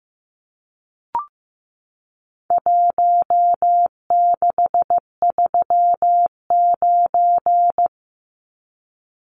Morse code, heard as a keyed tone, about 15 wpm, 700 Hz. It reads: 1639
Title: Numbers - Contest Style - 4-Digit 15wpm